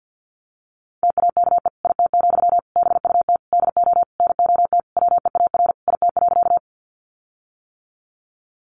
Morse code, heard as a keyed tone, about 50 wpm, 700 Hz. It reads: TAKE IT, BUT DO NOT WEAR IT.
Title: Sentences from Top 1500 Words 50wpm